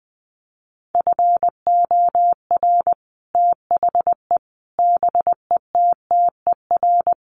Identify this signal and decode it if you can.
Morse code, heard as a keyed tone, about 20 wpm, 700 Hz. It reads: FOR THE BETTER
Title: Sets of 5 Words Encore2 20wpm